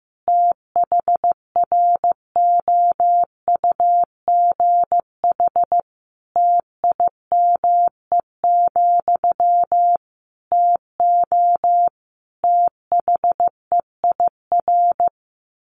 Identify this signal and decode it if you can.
Morse code, heard as a keyed tone, about 15 wpm, 700 Hz. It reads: THROUGH TIME, TO THEIR
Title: A Sound Of Thunder 15wpm